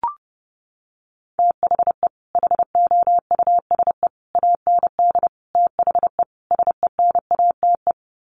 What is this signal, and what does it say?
Morse code, heard as a keyed tone, about 30 wpm, 700 Hz. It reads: THE HOUSE AND THE SENATE
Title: Sets of 5 Words Encore2 30wpm